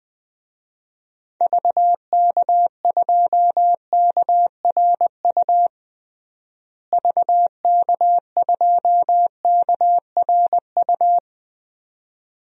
Morse code, heard as a keyed tone, about 20 wpm, 700 Hz. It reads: VK2KRU VK2KRU
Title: DXCC-100 Call Signs - 6 Characters - Repeated 2x - 20wpm